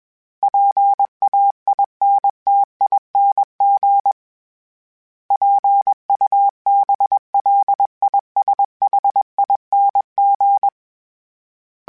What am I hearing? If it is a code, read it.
Morse code, heard as a keyed tone, about 21 wpm, 800 Hz. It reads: PAINTING PUBLISHING
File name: Special INGs / Special INGs-mp3s / Sngl Ing @20